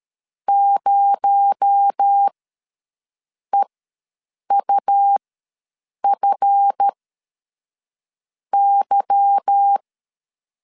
Morse code, heard as a keyed tone, about 13 wpm, spaced as for 6 wpm, 800 Hz. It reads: 0EUFY
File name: LEZIONI / 8 YJ45 / 8 Y ws 12wpm 3409